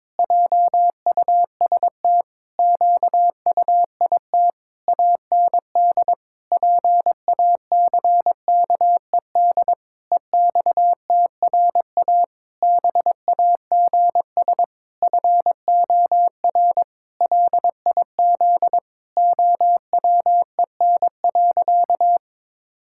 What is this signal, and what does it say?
Morse code, heard as a keyed tone, about 21 wpm, 700 Hz. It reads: JUST QUIT AND PACKED EXTRA BAGS FOR LIZ OWEN.
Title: Pangram Challenge v2 - 22wpm